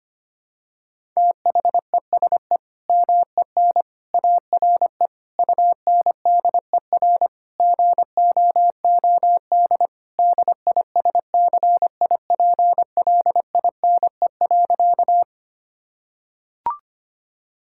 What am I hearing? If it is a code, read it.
Morse code, heard as a keyed tone, about 25 wpm, 700 Hz. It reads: THESE MEN ARE UNDER GOOD DISCIPLINE.
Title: Sentences from Top 1000 Words - Review Part 6 - 25wpm